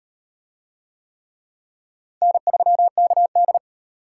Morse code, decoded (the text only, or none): N3KD